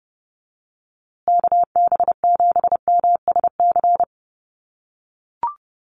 Morse code, heard as a keyed tone, about 30 wpm, 700 Hz. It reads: KB7MSC
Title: USA Call Signs Repeated 2x 30wpm